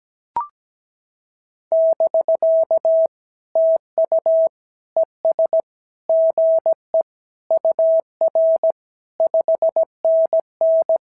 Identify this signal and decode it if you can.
Morse code, heard as a keyed tone, about 16 wpm, 650 Hz. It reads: <BK> TU ES GE UR 5NN
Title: SOTA QSO Practice 17wpm